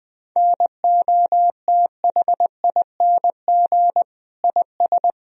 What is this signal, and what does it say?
Morse code, heard as a keyed tone, about 21 wpm, 700 Hz. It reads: NOTHING IS
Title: Sentences from Top 1000 Words - Review Part 4 - 20wpm